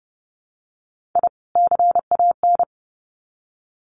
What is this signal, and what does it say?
Morse code, heard as a keyed tone, about 30 wpm, 700 Hz. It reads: I CAN
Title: Sets of 2 Words 30wpm